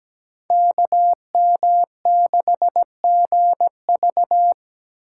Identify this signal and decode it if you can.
Morse code, heard as a keyed tone, about 17 wpm, 700 Hz. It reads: KM6GV
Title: USA Call Signs - 5 Characters - 4x Character Spacing - 17wpm